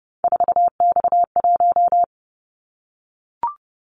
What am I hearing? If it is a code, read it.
Morse code, heard as a keyed tone, about 29 wpm, 700 Hz. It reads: 4X1
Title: Callsign Prefixes for Top 50 DXCC - 30wpm